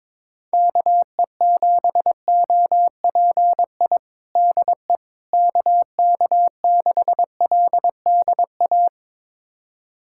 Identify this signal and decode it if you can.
Morse code, heard as a keyed tone, about 22 wpm, 700 Hz. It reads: KE7OPI DE KK6LDA
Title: USA Callsign DE Callsign - 6 Characters - 22wpm